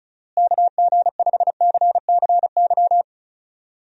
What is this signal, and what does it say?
Morse code, heard as a keyed tone, about 36 wpm, 700 Hz. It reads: KG5CCY
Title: USA Call Signs - 6 Characters - Repeated 2x - 35wpm